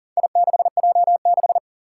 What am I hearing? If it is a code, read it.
Morse code, heard as a keyed tone, about 40 wpm, 700 Hz. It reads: I6J6
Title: Four Challenging Letter-Number Combinations 40wpm